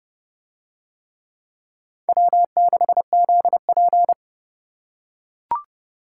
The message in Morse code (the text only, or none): W6ZP